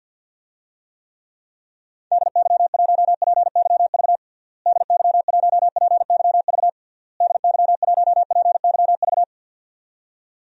Morse code, heard as a keyed tone, about 46 wpm, 700 Hz. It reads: DY1PYV DY1PYV DY1PYV
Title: DXCC-100 Call Signs - 6 Characters - Repeated 3x - 50wpm